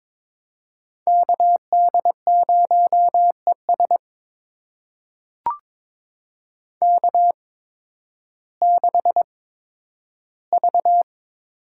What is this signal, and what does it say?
Morse code, heard as a keyed tone, about 22 wpm, spaced as for 12 wpm, 700 Hz. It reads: KD0ES K6V
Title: USA Call Signs - 5 Characters - 8x Character Spacing - 22wpm